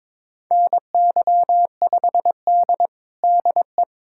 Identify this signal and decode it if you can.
Morse code, heard as a keyed tone, about 22 wpm, 700 Hz. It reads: NY5D DE
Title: USA Callsign DE Callsign 22wpm